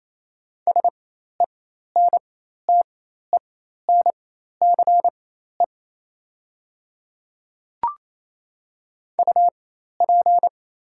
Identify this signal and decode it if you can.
Morse code, heard as a keyed tone, about 27 wpm, spaced as for 13 wpm, 700 Hz. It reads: SENTENCE UP
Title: 4x Character Spacing - Top 200 Words - 28wpm